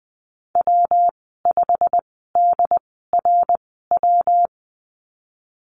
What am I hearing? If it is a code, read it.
Morse code, heard as a keyed tone, about 20 wpm, spaced as for 15 wpm, 700 Hz. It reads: W5DRW
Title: 173 Lesson 28g - Number 9 - Call Signs 20wpm